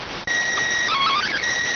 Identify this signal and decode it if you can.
SSTV (slow-scan television), PD 290.